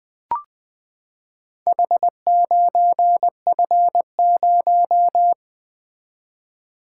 Morse code, heard as a keyed tone, about 20 wpm, 700 Hz. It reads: H9F0
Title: Four Letter-Number Combinations - 20wpm